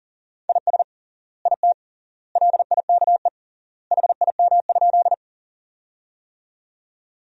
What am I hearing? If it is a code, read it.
Morse code, heard as a keyed tone, about 40 wpm, 700 Hz. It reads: IS IT LIKE HIM?
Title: 3x Word-Spacing - Sentences from Top 100 Words - 40wpm